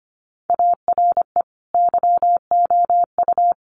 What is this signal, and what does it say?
Morse code, heard as a keyed tone, about 25 wpm, 700 Hz. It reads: ARE YOU
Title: Sentences from Top 700 Words 25wpm